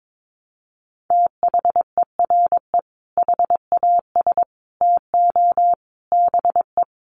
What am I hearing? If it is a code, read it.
Morse code, heard as a keyed tone, about 22 wpm, 700 Hz. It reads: THERE HAS TO BE
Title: Sets of 5 Words Encore 22wpm